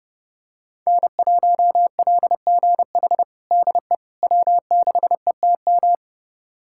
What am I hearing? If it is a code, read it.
Morse code, heard as a keyed tone, about 32 wpm, 700 Hz. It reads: N1LGH DE W6ETM
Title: USA Callsign DE Callsign - 5 Characters - 30wpm